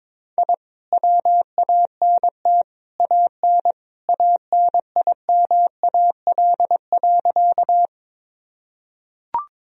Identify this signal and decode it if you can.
Morse code, heard as a keyed tone, about 22 wpm, 700 Hz. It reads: I WANT AN ANIMAL.
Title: Sentences from Top 1000 Words - Review Part 1 - 22wpm